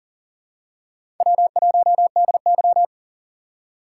Morse code, heard as a keyed tone, about 40 wpm, 700 Hz. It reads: W1DY
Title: USA Call Signs - 4 Characters - 2x Character Spacing - 40wpm